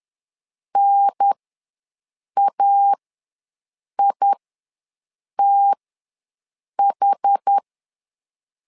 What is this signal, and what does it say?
Morse code, heard as a keyed tone, about 10 wpm, spaced as for 6 wpm, 800 Hz. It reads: NAITH
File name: LEZIONI / 4 Q ws 10wpm 2306